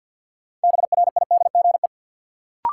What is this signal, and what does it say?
Morse code, heard as a keyed tone, about 49 wpm, 700 Hz. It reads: BRIDGE
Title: Sentences from Top 1200 Words - New Word Drill - 50wpm